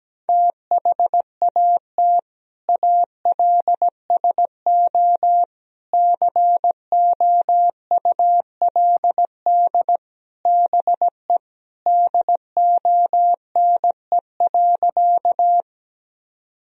Morse code, heard as a keyed tone, about 17 wpm, 700 Hz. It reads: THAT ALSO COULD BE DONE.
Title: Sentences from Top 1000 Words - Review Part 1 - 17wpm